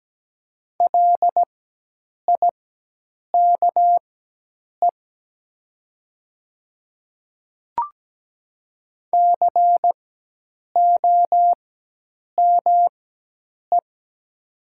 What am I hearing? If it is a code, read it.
Morse code, heard as a keyed tone, about 17 wpm, spaced as for 8 wpm, 700 Hz. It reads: LIKE COME
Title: 4x Character Spacing - Top 100 Words - 17wpm